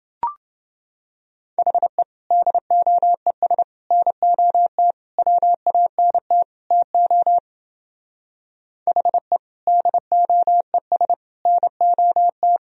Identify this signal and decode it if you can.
Morse code, heard as a keyed tone, about 28 wpm, 700 Hz. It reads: HE DOES NOT WANT TO HE DOES NOT